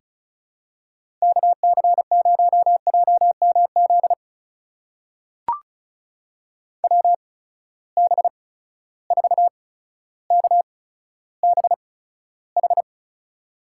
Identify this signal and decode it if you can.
Morse code, heard as a keyed tone, about 36 wpm, spaced as for 15 wpm, 700 Hz. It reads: KC0JMZ WB4KBH
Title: USA Call Signs - 6 Characters - 8x Character Spacing - 35wpm